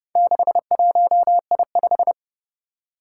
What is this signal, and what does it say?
Morse code, heard as a keyed tone, about 31 wpm, 700 Hz. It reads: 61I5
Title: Four Challenging Letter-Number Combinations 30wpm